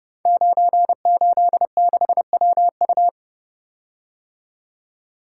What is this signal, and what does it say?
Morse code, heard as a keyed tone, about 30 wpm, 700 Hz. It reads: 986WU
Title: Code Groups - 5 Letter:Numbers - 30wpm